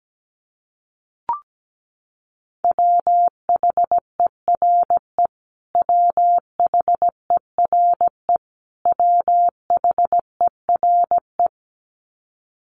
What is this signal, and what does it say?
Morse code, heard as a keyed tone, about 17 wpm, 700 Hz. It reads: WHERE WHERE WHERE
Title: Top 300 Words Repeated 17wpm